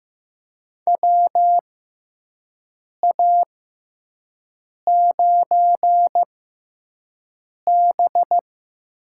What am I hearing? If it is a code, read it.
Morse code, heard as a keyed tone, about 15 wpm, spaced as for 5 wpm, 700 Hz. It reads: WA9B
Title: USA Call Signs - 6 Characters - 6x Character Spacing - 15wpm